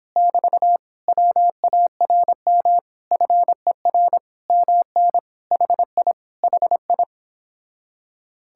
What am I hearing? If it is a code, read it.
Morse code, heard as a keyed tone, about 26 wpm, 700 Hz. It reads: <BT> WARM FER MN HI HI